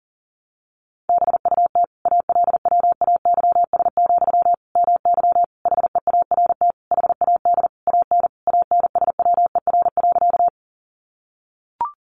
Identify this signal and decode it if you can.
Morse code, heard as a keyed tone, about 40 wpm, 700 Hz. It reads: BUT ALWAYS, MY HEART HAD AN ANSWER.
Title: Sentences from Top 1200 Words 40wpm